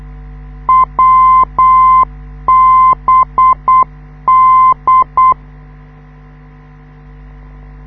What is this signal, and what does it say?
Morse code, heard as a keyed tone, about 8 wpm, 1040 Hz. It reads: WBD
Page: this is how the NDB of the Wiesbaden US Army Airfield at Erbenheim sounds like, when you live 2 km away